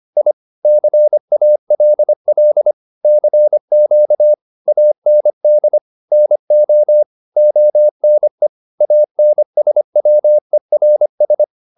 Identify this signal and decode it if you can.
Morse code, heard as a keyed tone, about 25 wpm, 600 Hz. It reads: I CALL CQ AND NO ONE ANSWERS